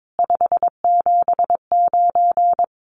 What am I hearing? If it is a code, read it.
Morse code, heard as a keyed tone, about 22 wpm, 700 Hz. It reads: 579
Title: Common QSO Phrases Encore 22wpm